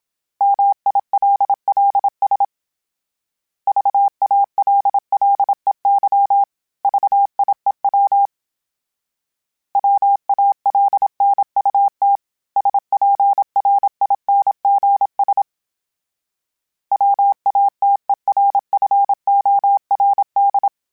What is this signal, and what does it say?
Morse code, heard as a keyed tone, about 27 wpm, 800 Hz. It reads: MILLS VALLEY VIEW WALNUT SPRINGS WATERFORD
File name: fromSMs / FromSMs mp3s / Cities@25